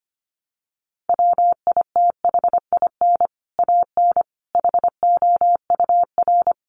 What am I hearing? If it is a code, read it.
Morse code, heard as a keyed tone, about 25 wpm, 700 Hz. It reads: WITHIN AN HOUR
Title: Sentences from Top 700 Words 25wpm